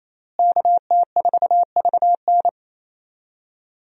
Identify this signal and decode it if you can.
Morse code, heard as a keyed tone, about 28 wpm, 700 Hz. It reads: KT4VN
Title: USA Call Signs - 5 Characters - Repeated 2x - 28wpm